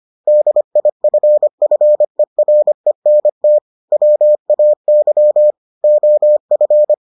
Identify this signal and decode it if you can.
Morse code, heard as a keyed tone, about 25 wpm, 600 Hz. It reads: DIFFERENT WAY OF